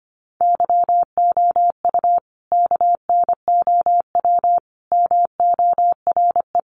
Morse code, heard as a keyed tone, about 25 wpm, 700 Hz. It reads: YOU KNOW MORE
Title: Sentences from Top 100 Words 25wpm